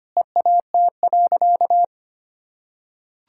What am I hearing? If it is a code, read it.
Morse code, heard as a keyed tone, about 25 wpm, 700 Hz. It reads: EAT.
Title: Sentences from Top 1600 Words 25wpm